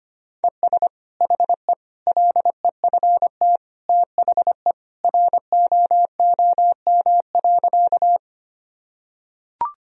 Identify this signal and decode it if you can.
Morse code, heard as a keyed tone, about 25 wpm, 700 Hz. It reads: ES HE LEFT THE ROOM.
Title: Sentences from Top 300 Words and Abbreviations 25wpm